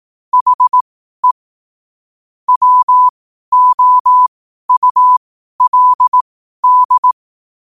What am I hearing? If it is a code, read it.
Morse code, heard as a keyed tone, about 18 wpm, spaced as for 12 wpm, 1000 Hz. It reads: HE WOULD
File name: qotdfiles / 12wpm